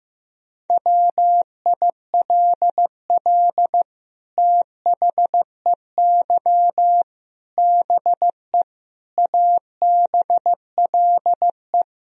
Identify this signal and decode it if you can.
Morse code, heard as a keyed tone, about 15 wpm, 700 Hz. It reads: WILL THEY BE ABLE
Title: Sentences from Top 1000 Words - Review Part 2 - 15wpm